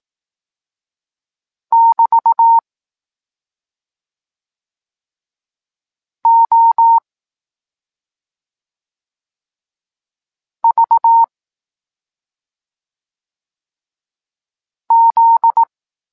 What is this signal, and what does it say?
Morse code, heard as a keyed tone, about 18 wpm, 920 Hz. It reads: <BT> O V Z
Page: Tekst 23.3 – Ciąg dalszy doskonalenia odbioru poznanych liter